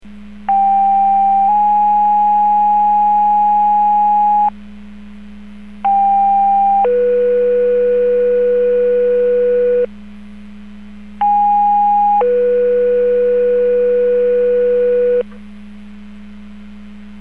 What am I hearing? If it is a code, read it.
Two-tone page, heard as a one-second tone, then a three-second tone, about 790 then 830 Hz, about 800 then 490 Hz, about 820 then 490 Hz. It is heard from 0.5 to 15.2 s.